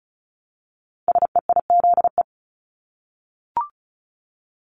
Morse code, heard as a keyed tone, about 36 wpm, 700 Hz. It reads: SEIZE
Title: Multi-letter Phonemes Repeated 3x 35wpm